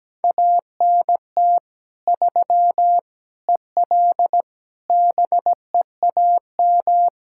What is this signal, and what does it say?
Morse code, heard as a keyed tone, about 17 wpm, 700 Hz. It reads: ANT 3 EL BEAM
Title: Practice QSO - Exchange 2 - Info - 17wpm